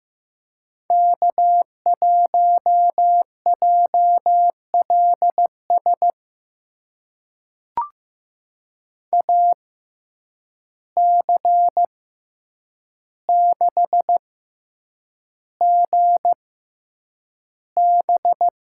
Morse code, heard as a keyed tone, about 15 wpm, spaced as for 8 wpm, 700 Hz. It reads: K1JLS AC6GB
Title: USA Call Signs - 5 Characters - 6x Character Spacing - 15wpm